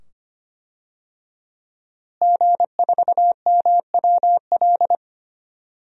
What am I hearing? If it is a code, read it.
Morse code, heard as a keyed tone, about 25 wpm, 700 Hz. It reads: G4MWL
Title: DXCC-100 Call Signs - 5 Characters - Repeated 2x - 25wpm